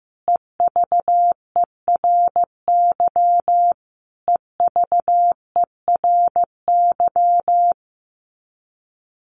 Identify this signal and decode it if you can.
Morse code, heard as a keyed tone, about 15 wpm, 700 Hz. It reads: EVERY EVERY
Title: Top 300 Words Repeated 15wpm